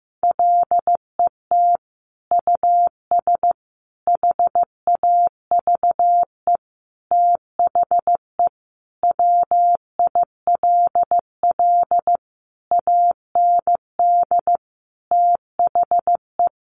Morse code, heard as a keyed tone, about 15 wpm, 700 Hz. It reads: LET US HAVE THE WILL AND THE
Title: Sentences from Top 1300 Words 15wpm